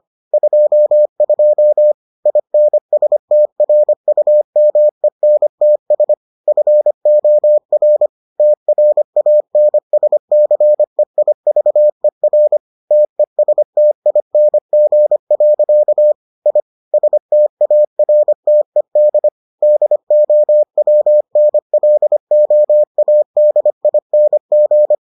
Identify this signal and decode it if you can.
Morse code, heard as a keyed tone, about 25 wpm, 600 Hz. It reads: 22 INSTRUMENTS FOR TRANSCEIVER TESTING. I STARTED DOWNLOADING